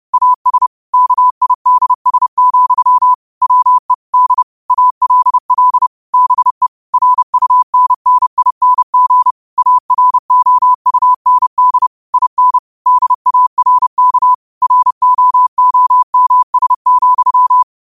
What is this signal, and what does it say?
Morse code, heard as a keyed tone, about 30 wpm, 1000 Hz. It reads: AS KIDS, WE D ALL BE RUNNING AROUND IN DARK ROOMS,
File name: qotdfiles / 30wpm